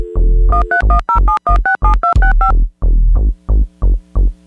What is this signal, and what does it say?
DTMF, heard as telephone touch tones, tones of about 120 ms. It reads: 13107167265